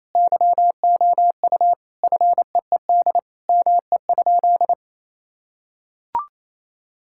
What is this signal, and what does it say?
Morse code, heard as a keyed tone, about 27 wpm, 700 Hz. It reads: YOU FEED ME?
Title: Sentences from Top 1600 Words 28wpm